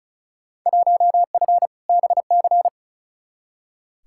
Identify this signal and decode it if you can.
Morse code, heard as a keyed tone, about 36 wpm, 700 Hz. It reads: 1F BC
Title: Field Day Exchanges Encore - 35wpm